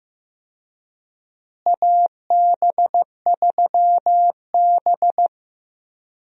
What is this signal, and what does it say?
Morse code, heard as a keyed tone, about 16 wpm, 700 Hz. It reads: AB3B
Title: USA Call Signs - 4 Characters - Repeated 3x - 15wpm